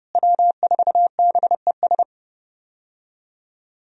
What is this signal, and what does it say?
Morse code, heard as a keyed tone, about 29 wpm, 700 Hz. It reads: W4BES
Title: USA Call Signs - 5 Characters - Repeated 2x - 30wpm